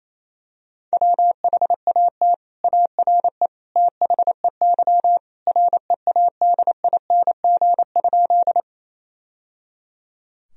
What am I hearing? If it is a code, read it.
Morse code, heard as a keyed tone, about 27 wpm, 700 Hz. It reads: WHAT ARE THEY READING?
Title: Sentences from Top 1000 Words - Review Part 3 - 28wpm